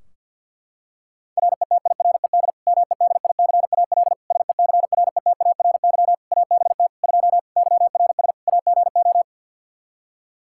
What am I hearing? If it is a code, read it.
Morse code, heard as a keyed tone, about 52 wpm, 700 Hz. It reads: RETIRED MEDICAL SECRETARY ABT 2 YRS AGO
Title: Practice QSO - Exchange 3 - More Info - 50wpm